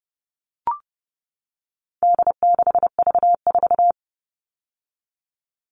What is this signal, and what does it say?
Morse code, heard as a keyed tone, about 30 wpm, 700 Hz. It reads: D6V4
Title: All Cut Numbers - 4-Digit 30wpm